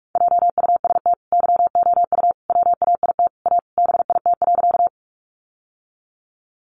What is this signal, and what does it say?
Morse code, heard as a keyed tone, about 44 wpm, 700 Hz. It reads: JUST YOU WAIT A BIT.
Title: Sentences from Top 900 Words 45wpm